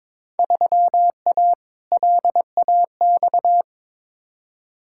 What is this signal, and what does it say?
Morse code, heard as a keyed tone, about 22 wpm, 700 Hz. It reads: 3A LAX
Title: Field Day Exchanges - 22wpm